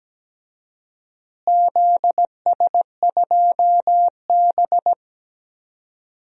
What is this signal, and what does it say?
Morse code, heard as a keyed tone, about 17 wpm, 700 Hz. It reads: ZS2B
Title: DXCC-100 Call Signs - 4 Characters - Repeated 2x - 17wpm